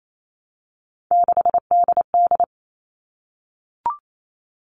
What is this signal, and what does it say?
Morse code, heard as a keyed tone, about 28 wpm, 700 Hz. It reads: 6DD
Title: All Cut Numbers - 3-Digit 28wpm